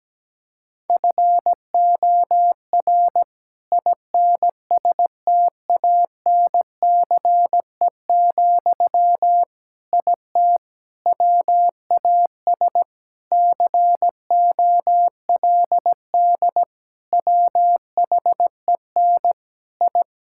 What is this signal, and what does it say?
Morse code, heard as a keyed tone, about 17 wpm, 700 Hz. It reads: FOR INSTANCE, IT WAS COLD WHEN I
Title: Sentences from Top 1000 Words - Review Part 5 - 17wpm